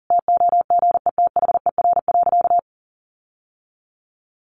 Morse code, heard as a keyed tone, about 40 wpm, 700 Hz. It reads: TOGETHER.
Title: Sentences from Top 1500 Words 40wpm